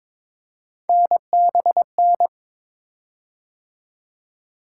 Morse code, heard as a keyed tone, about 22 wpm, 700 Hz. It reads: NBN
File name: practice / 22wpm / Three Letter-Number Combinations - 22wpm